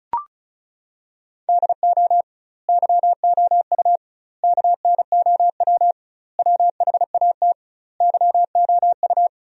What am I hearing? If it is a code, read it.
Morse code, heard as a keyed tone, about 34 wpm, 700 Hz. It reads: DO YOU KNOW WHAT YOU
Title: 2x Word-Spacing - Sentences from Top 100 Words - 35wpm